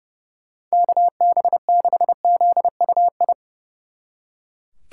Morse code, heard as a keyed tone, about 35 wpm, 700 Hz. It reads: KB6ZUI
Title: Speed Racing - USA Call Signs 30wpm